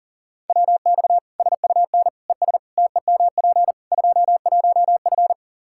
Morse code, heard as a keyed tone, about 40 wpm, 700 Hz. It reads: WX SUN ES TEMP 21F
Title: Practice QSO - Exchange 2 - Info - 40wpm